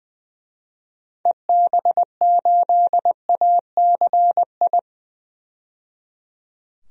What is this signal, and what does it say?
Morse code, heard as a keyed tone, about 20 wpm, 700 Hz. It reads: EB8ACI
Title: DXCC-100 Call Signs - 6 Characters - 20wpm